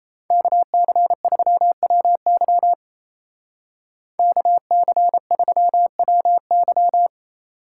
Morse code, heard as a keyed tone, about 35 wpm, 700 Hz. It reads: KC3WY KC3WY
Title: Speed Racing - USA 2x2 Call Signs 28wpm